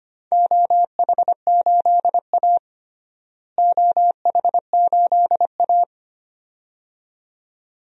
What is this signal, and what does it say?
Morse code, heard as a keyed tone, about 25 wpm, 700 Hz. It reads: OH8A OH8A
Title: DXCC-100 Call Signs - 4 Characters - Repeated 2x - 25wpm